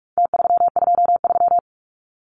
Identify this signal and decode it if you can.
Morse code, heard as a keyed tone, about 44 wpm, 700 Hz. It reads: T323
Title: Numbers - Contest Style - 4-Digit 45wpm